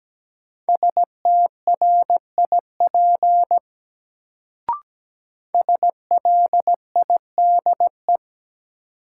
Morse code, heard as a keyed tone, about 17 wpm, 700 Hz. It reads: STRIP SLIDE
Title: Sentences from Top 1600 Words - New Word Drill Alt - 17wpm